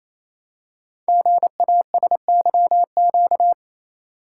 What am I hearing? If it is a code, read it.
Morse code, heard as a keyed tone, about 27 wpm, 700 Hz. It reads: GASYQ
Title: Code Groups - 5 Letter:Numbers - 28wpm